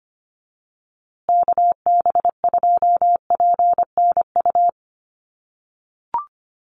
Morse code, heard as a keyed tone, about 25 wpm, 700 Hz. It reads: KB2PNU
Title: USA Call Signs Repeated 2x 25wpm